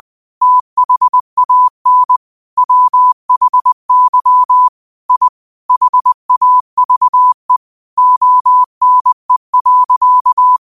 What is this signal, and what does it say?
Morse code, heard as a keyed tone, about 20 wpm, 1000 Hz. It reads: THAN WHY I HAVE ONE.
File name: qotdfiles / 20wpm